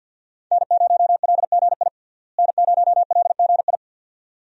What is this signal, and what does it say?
Morse code, heard as a keyed tone, about 50 wpm, 700 Hz. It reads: N0LGI N0LGI
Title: USA Call Signs - 5 Characters - Repeated 2x - 50wpm